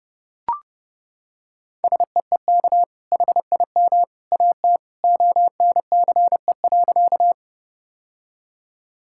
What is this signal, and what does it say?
Morse code, heard as a keyed tone, about 30 wpm, 700 Hz. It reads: SEEK HIM AT ONCE.
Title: Sentences from Top 1600 Words 30wpm